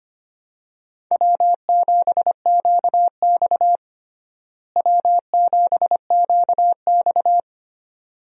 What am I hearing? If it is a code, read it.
Morse code, heard as a keyed tone, about 25 wpm, 700 Hz. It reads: W7QX W7QX
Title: USA Call Signs - 4 Characters - Repeated 2x - 25wpm